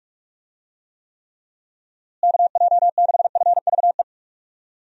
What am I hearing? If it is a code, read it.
Morse code, heard as a keyed tone, about 46 wpm, 700 Hz. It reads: KJ6FVE
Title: USA Call Signs - 6 Characters - Repeated 3x - 45wpm